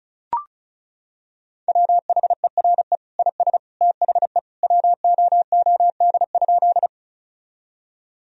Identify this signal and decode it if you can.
Morse code, heard as a keyed tone, about 34 wpm, 700 Hz. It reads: WHERE IS THE WOOD?
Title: Sentences from Top 1000 Words - Review Part 2 - 35wpm